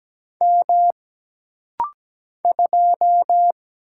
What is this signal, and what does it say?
Morse code, heard as a keyed tone, about 17 wpm, 700 Hz. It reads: M 2
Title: Single Letter-Number Alt - Rapid-Fire 17wpm